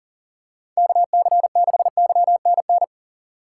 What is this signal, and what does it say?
Morse code, heard as a keyed tone, about 38 wpm, 700 Hz. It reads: KC6YNN
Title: USA Call Signs - 6 Characters - 2x Character Spacing - 40wpm